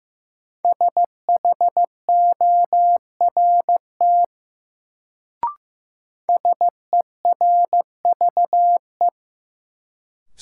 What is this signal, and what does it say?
Morse code, heard as a keyed tone, about 15 wpm, 700 Hz. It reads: SHORT SERVE
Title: Sentences from Top 400 Words - New Word Drill - 15wpm